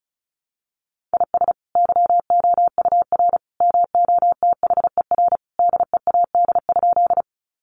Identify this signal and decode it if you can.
Morse code, heard as a keyed tone, about 34 wpm, 700 Hz. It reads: IS YOUR MOTHER DEAD?
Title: Sentences from Top 1500 Words 35wpm